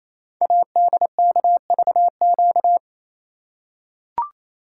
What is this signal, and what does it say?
Morse code, heard as a keyed tone, about 27 wpm, 700 Hz. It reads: ADKVQ
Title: Code Groups - 5 Letters - 28wpm